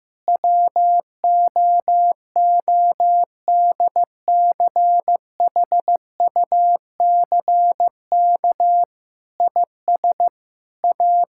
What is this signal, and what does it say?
Morse code, heard as a keyed tone, about 15 wpm, 700 Hz. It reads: WOODCHUCK IS A
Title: Pangram Challenge v2 - 15wpm